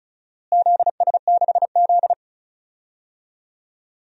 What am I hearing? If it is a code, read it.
Morse code, heard as a keyed tone, about 36 wpm, 690 Hz. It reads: ZS6Z
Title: DXCC-100 Call Signs - 4 Characters - Repeated 3x - 35wpm